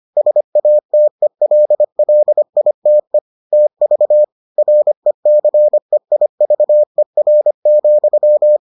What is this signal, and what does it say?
Morse code, heard as a keyed tone, about 25 wpm, 600 Hz. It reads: SATELLITE TV RECEIVER,